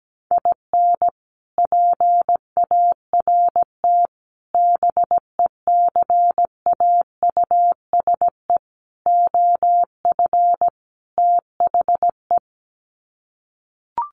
Morse code, heard as a keyed tone, about 17 wpm, 700 Hz. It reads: IN PART BECAUSE OF THE
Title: Sets of 5 Words Encore2 17wpm